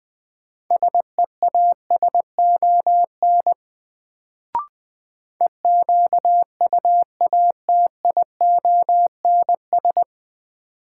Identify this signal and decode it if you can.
Morse code, heard as a keyed tone, about 21 wpm, 700 Hz. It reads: SEASON EQUATIONS
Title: Sets of 5 Words - New Word Drill - 20wpm